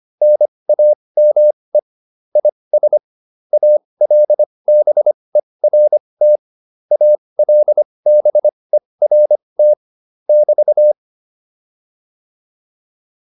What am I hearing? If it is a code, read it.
Morse code, heard as a keyed tone, about 25 wpm, spaced as for 20 wpm, 600 Hz. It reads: NAME IS ALBERT ALBERT <BT>
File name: QSO109a 20